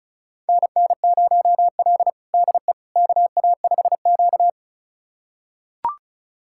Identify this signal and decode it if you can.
Morse code, heard as a keyed tone, about 36 wpm, 700 Hz. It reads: NN0L DE KA5Q
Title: USA Callsign DE Callsign - 4 Characters - 35wpm